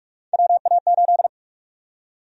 Morse code, heard as a keyed tone, about 44 wpm, 700 Hz. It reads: WA8
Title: USA Call Sign Prefixes - 45wpm